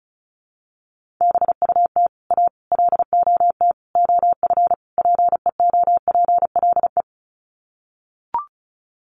Morse code, heard as a keyed tone, about 34 wpm, 700 Hz. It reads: BUT A LOT OF PEOPLE
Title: Sets of 5 Words Encore2 35wpm